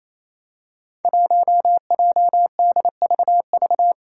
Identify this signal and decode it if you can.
Morse code, heard as a keyed tone, about 27 wpm, 700 Hz. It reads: 1JDVV
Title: Five Challenging Letter-Number Combinations 28wpm